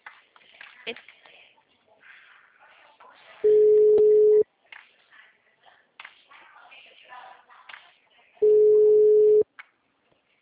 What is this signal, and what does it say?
Ringback tone (Europe, 425 Hz), with speech.